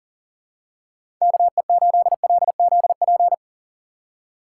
Morse code, heard as a keyed tone, about 37 wpm, 700 Hz. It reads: KE8LZP
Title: USA Call Signs - 6 Characters - 4x Character Spacing - 40wpm